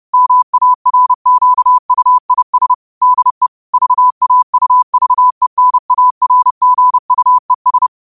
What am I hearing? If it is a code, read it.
Morse code, heard as a keyed tone, about 30 wpm, 1000 Hz. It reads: MARQUIS DE VAUVENARGUES
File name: qotdfiles / 30wpm